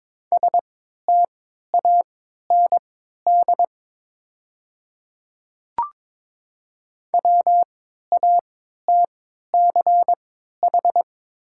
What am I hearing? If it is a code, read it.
Morse code, heard as a keyed tone, about 23 wpm, spaced as for 13 wpm, 700 Hz. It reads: STAND WATCH
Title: 3x Character Spacing - Top 300 Words - 22wpm